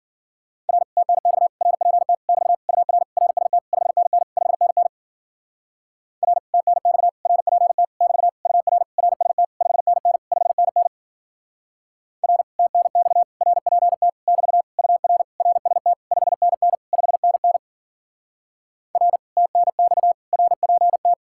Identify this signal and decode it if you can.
Morse code, heard as a keyed tone, about 52 wpm, 700 Hz. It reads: R TNX RPT <BT> UR RST 5NN 5NN R TNX RPT <BT> UR RST 5NN 5NN R TNX RPT <BT> UR RST 5NN 5NN R TNX RPT